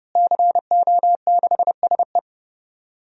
Morse code, heard as a keyed tone, about 30 wpm, 700 Hz. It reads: CO6SE
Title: DXCC-100 Call Signs - 5 Characters - Repeated 2x - 30wpm